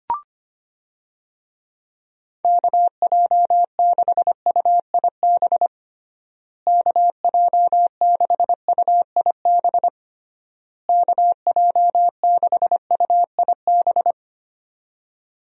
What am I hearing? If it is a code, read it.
Morse code, heard as a keyed tone, about 25 wpm, 700 Hz. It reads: KJ6UIB KJ6UIB KJ6UIB
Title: USA Call Signs - 6 Characters - Repeated 3x - 25wpm